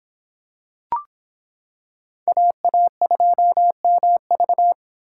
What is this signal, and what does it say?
Morse code, heard as a keyed tone, about 25 wpm, 700 Hz. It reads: AA2MV